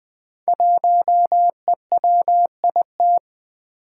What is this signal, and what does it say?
Morse code, heard as a keyed tone, about 20 wpm, 700 Hz. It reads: 1EWIT
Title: Five Challenging Letter-Number Combinations 20wpm